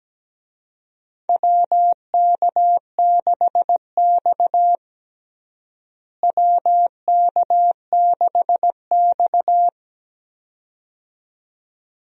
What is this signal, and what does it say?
Morse code, heard as a keyed tone, about 17 wpm, 700 Hz. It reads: WK6X WK6X
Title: USA Call Signs - 4 Characters - Repeated 2x - 17wpm